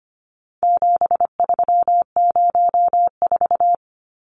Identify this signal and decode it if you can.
Morse code, heard as a keyed tone, about 25 wpm, 700 Hz. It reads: 7304
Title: Numbers - 4-Digit 25wpm